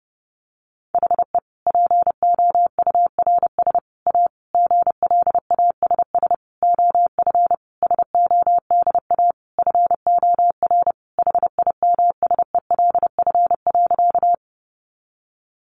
Morse code, heard as a keyed tone, about 30 wpm, 700 Hz. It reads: HE POURS A GLASS OF SODA FOR HIMSELF.
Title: Additional Easy Phrases 30wpm